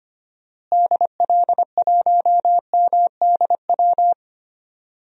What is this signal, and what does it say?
Morse code, heard as a keyed tone, about 25 wpm, 700 Hz. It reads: DL1MDW
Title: DXCC-100 Call Signs - 6 Characters - Repeated 2x - 25wpm